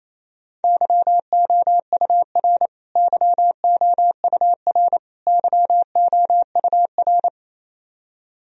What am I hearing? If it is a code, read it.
Morse code, heard as a keyed tone, about 28 wpm, 700 Hz. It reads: YOUR YOUR YOUR
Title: Top 100 Words Repeated 28wpm